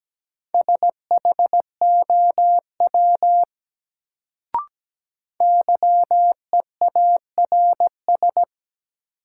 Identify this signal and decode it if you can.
Morse code, heard as a keyed tone, about 17 wpm, 700 Hz. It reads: SHOW YEARS
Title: Sentences from Top 200 Words - New Word Drill - 17wpm